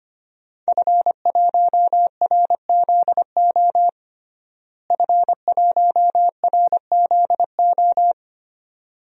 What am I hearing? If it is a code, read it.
Morse code, heard as a keyed tone, about 25 wpm, 700 Hz. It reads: F1RZO F1RZO
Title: DXCC-100 Call Signs - 5 Characters - Repeated 2x - 25wpm